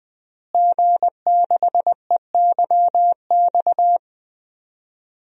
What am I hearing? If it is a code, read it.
Morse code, heard as a keyed tone, about 20 wpm, 700 Hz. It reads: G6EYX
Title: DXCC-100 Call Signs - 5 Characters - Repeated 3x - 20wpm